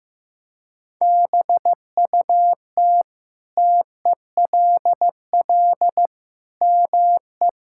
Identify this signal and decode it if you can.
Morse code, heard as a keyed tone, about 15 wpm, 700 Hz. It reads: BUT TELL ME
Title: Sentences from Top 1000 Words - Review Part 6 - 15wpm